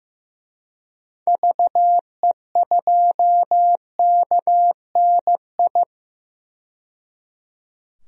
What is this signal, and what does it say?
Morse code, heard as a keyed tone, about 15 wpm, 700 Hz. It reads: VE2KNI
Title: DXCC-100 Call Signs - 6 Characters - 15wpm